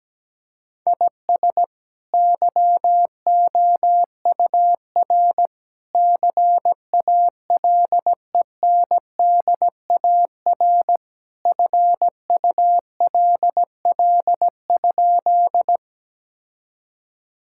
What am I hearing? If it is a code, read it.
Morse code, heard as a keyed tone, about 17 wpm, 700 Hz. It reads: IS YOUR CALENDAR FULL?
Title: Sentences from Top 1600 Words 17wpm